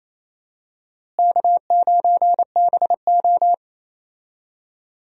Morse code, heard as a keyed tone, about 28 wpm, 700 Hz. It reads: K9BO
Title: USA Call Signs - 4 Characters - Repeated 2x - 28wpm